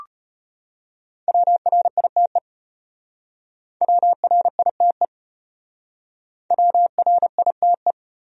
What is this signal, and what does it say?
Morse code, heard as a keyed tone, about 32 wpm, 700 Hz. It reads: WRITE WRITE WRITE